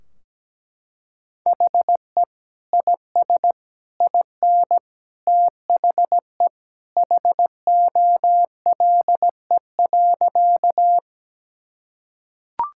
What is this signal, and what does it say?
Morse code, heard as a keyed tone, about 17 wpm, 700 Hz. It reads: HE IS IN THE HOLE.
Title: Sentences from Top 1600 Words 17wpm